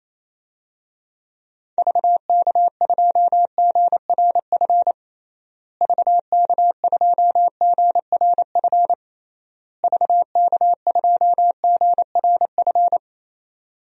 Morse code, heard as a keyed tone, about 27 wpm, 700 Hz. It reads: VK2GRF VK2GRF VK2GRF
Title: DXCC-100 Call Signs - 6 Characters - Repeated 3x - 28wpm